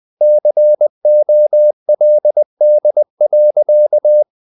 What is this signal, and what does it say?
Morse code, heard as a keyed tone, about 20 wpm, 600 Hz. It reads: COLD.
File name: the old man and the sea / the old man and the sea part 1 20wpm